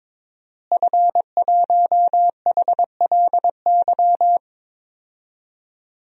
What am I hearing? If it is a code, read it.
Morse code, heard as a keyed tone, about 22 wpm, 700 Hz. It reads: F1HLY
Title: DXCC-100 Call Signs - 5 Characters - Repeated 3x - 22wpm